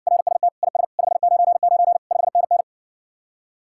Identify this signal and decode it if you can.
Morse code, heard as a keyed tone, about 60 wpm, 700 Hz. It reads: RST IS 599 5NN